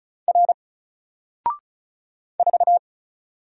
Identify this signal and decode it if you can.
Morse code, heard as a keyed tone, about 33 wpm, 700 Hz. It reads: R 4
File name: Single-Letter-Number-35wpm